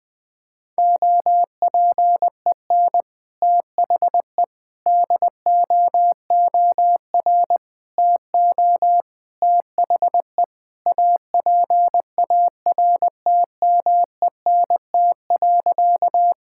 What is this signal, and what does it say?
Morse code, heard as a keyed tone, about 20 wpm, 700 Hz. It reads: OPEN THE DOOR TO THE APARTMENT.
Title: Sentences from Top 1000 Words - Review Part 6 - 20wpm